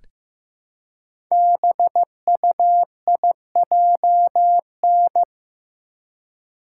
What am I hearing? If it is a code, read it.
Morse code, heard as a keyed tone, about 15 wpm, 700 Hz. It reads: BUIJN
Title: Code Groups - 5 Letters - 15wpm